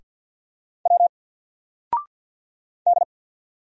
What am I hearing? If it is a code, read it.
Morse code, heard as a keyed tone, about 48 wpm, 700 Hz. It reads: W D
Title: Single Letter-Number Alt 50wpm